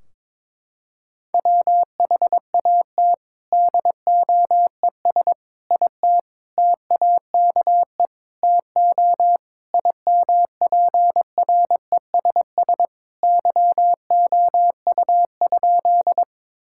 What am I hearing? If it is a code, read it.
Morse code, heard as a keyed tone, about 22 wpm, 700 Hz. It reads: WHAT DOES IT TAKE TO IMPRESS YOU?
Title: Sentences from Top 1600 Words 22wpm